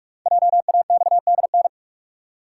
Morse code, heard as a keyed tone, about 44 wpm, 700 Hz. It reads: JAXDN
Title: Code Groups - 5 Letters - 45wpm